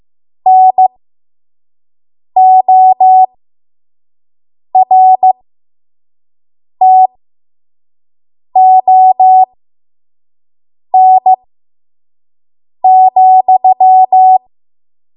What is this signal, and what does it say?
Morse code, heard as a keyed tone, about 15 wpm, spaced as for 5 wpm, 750 Hz. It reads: NORTON,
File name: Morse / 05WPM